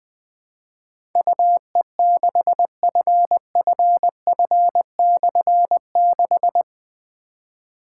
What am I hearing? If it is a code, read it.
Morse code, heard as a keyed tone, about 20 wpm, 700 Hz. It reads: UE6FFF/6
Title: Difficult Call Signs 20wpm